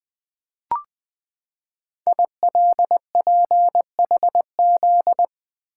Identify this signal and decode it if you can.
Morse code, heard as a keyed tone, about 20 wpm, 700 Hz. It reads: ILPHZ
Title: Code Groups - 5 Letter:Numbers - 20wpm